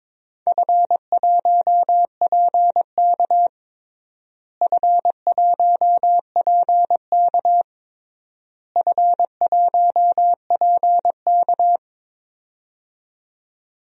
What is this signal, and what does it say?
Morse code, heard as a keyed tone, about 22 wpm, 700 Hz. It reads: F1PK F1PK F1PK
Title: DXCC-100 Call Signs - 4 Characters - Repeated 3x - 22wpm